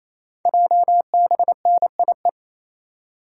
Morse code, heard as a keyed tone, about 29 wpm, 700 Hz. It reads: JBNIE
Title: Code Groups - 5 Letters - 28wpm